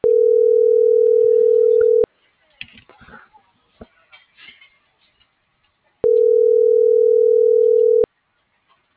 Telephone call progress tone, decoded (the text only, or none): ringback tone